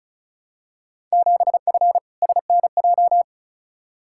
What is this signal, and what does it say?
Morse code, heard as a keyed tone, about 34 wpm, 700 Hz. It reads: 7F SNJ
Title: Field Day Exchanges Encore - 35wpm